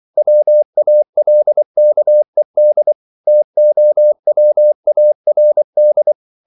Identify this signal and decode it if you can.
Morse code, heard as a keyed tone, about 24 wpm, 600 Hz. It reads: WALKED TOWARD